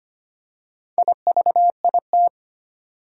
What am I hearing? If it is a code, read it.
Morse code, heard as a keyed tone, about 25 wpm, 700 Hz. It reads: IVIT
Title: Four Challenging Letter-Number Combinations 25wpm